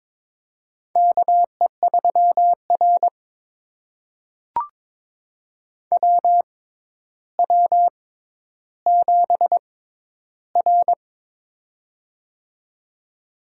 Morse code, heard as a keyed tone, about 22 wpm, spaced as for 11 wpm, 700 Hz. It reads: KE3R WW7R
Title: USA Call Signs - 4 Characters - 6x Character Spacing - 22wpm